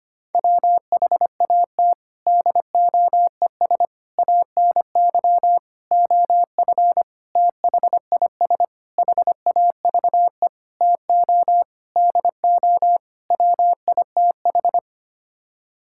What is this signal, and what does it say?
Morse code, heard as a keyed tone, about 25 wpm, 700 Hz. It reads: WHAT DOES ANY OF THIS HAVE TO DO WITH
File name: Sets of 9 Words - 25wpm